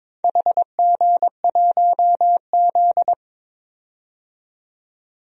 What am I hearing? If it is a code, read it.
Morse code, heard as a keyed tone, about 23 wpm, 700 Hz. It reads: HG1Z
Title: DXCC-100 Call Signs - 4 Characters - 22wpm